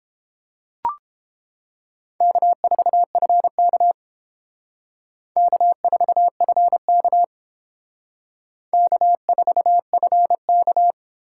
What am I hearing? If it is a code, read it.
Morse code, heard as a keyed tone, about 28 wpm, 700 Hz. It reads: K4FK K4FK K4FK